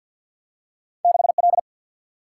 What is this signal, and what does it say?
Morse code, heard as a keyed tone, about 50 wpm, 700 Hz. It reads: 6 L
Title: Single-Letter Single-Number 50wpm